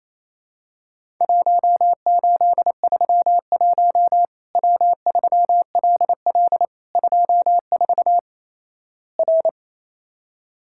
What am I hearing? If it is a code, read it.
Morse code, heard as a keyed tone, about 28 wpm, 700 Hz. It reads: 1831 W3LL 24 R
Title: Worked All Europe DX Contest - QTC Exchange - 28wpm